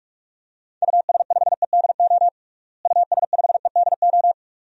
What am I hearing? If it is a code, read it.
Morse code, heard as a keyed tone, about 42 wpm, 700 Hz. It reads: US5EDO US5EDO
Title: DXCC-100 Call Signs - 6 Characters - Repeated 2x - 45wpm